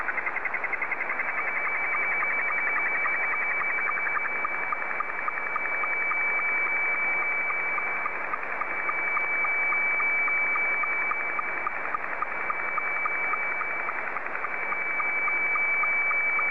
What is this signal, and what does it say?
SSTV (slow-scan television), Scottie 2.
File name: sstv / S2-ZL2TJM